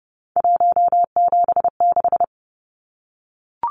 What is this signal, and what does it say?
Morse code, heard as a keyed tone, about 30 wpm, 700 Hz. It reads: 176
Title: Numbers - 3-Digit 30wpm